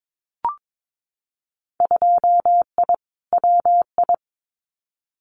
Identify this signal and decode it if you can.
Morse code, heard as a keyed tone, about 22 wpm, 700 Hz. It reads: 2I WI
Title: Winter Field Day Exchanges 22wpm